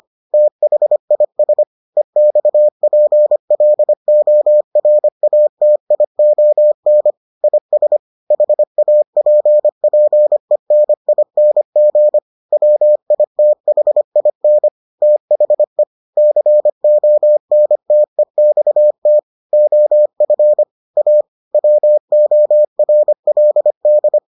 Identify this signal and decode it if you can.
Morse code, heard as a keyed tone, about 25 wpm, 600 Hz. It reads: THIS EXPLORATION IS HAPPENING WITHIN THE CONTEXT OF A WORLD